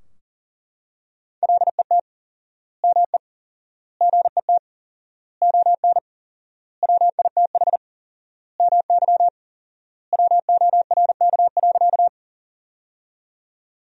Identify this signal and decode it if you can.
Morse code, heard as a keyed tone, about 40 wpm, 700 Hz. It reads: LET ME GET ON WITH MY WORK.
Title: 4x Word-Spacing - Sentences from Top 300 Words - 40wpm